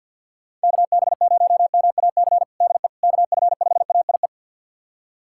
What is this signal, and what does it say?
Morse code, heard as a keyed tone, about 51 wpm, 700 Hz. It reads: KB0MAC DE KF5AIE
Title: USA Callsign DE Callsign - 6 Characters - 50wpm